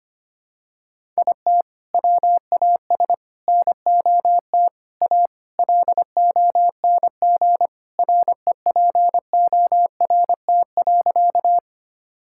Morse code, heard as a keyed tone, about 25 wpm, 700 Hz. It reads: IT WAS NOT A LONG REPORT.
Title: Sentences from Top 1000 Words - Review Part 6 - 25wpm